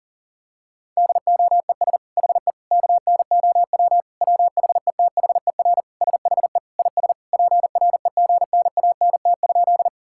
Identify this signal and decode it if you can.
Morse code, heard as a keyed tone, about 40 wpm, 700 Hz. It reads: DOES HE KNOW WHETHER SHE IS PREGNANT?
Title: Sentences from Top 1000 Words - Review Part 6 - 40wpm